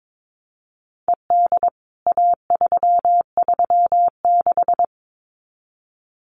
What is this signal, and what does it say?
Morse code, heard as a keyed tone, about 22 wpm, 700 Hz. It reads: ED A336
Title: CWT Exchange 22wpm